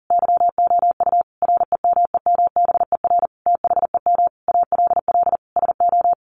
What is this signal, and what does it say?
Morse code, heard as a keyed tone, about 37 wpm, 700 Hz. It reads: YOU REMEMBER THEM ALL SO
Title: Sentences from Top 1300 Words 40wpm